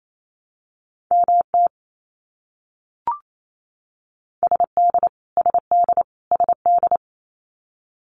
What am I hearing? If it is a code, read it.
Morse code, heard as a keyed tone, about 28 wpm, 700 Hz. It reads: MT SD SD SD
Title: US States Repeated 3x 28wpm